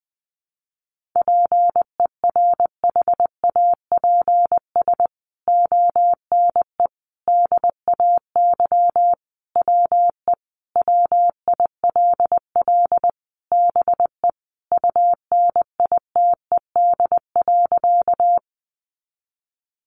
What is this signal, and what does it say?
Morse code, heard as a keyed tone, about 20 wpm, 700 Hz. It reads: PERHAPS ONE DAY WE WILL BE UNITED.
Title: Sentences from Top 600 Words 20wpm